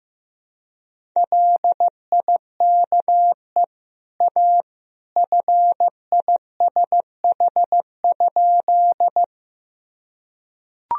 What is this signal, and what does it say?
Morse code, heard as a keyed tone, about 15 wpm, 700 Hz. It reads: LIKE A FISH?
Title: Sentences from Top 1000 Words - Review Part 1 - 15wpm